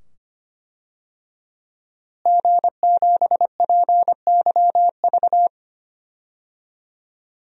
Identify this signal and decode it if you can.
Morse code, heard as a keyed tone, about 25 wpm, 700 Hz. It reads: G7PYV
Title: DXCC-100 Call Signs - 5 Characters - Repeated 2x - 25wpm